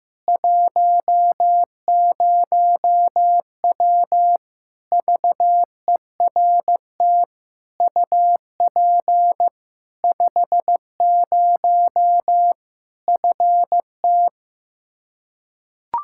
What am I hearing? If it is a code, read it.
Morse code, heard as a keyed tone, about 15 wpm, 700 Hz. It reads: 10W VERT UP 50 FT
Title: Practice QSO - Exchange 2 - Info - 15wpm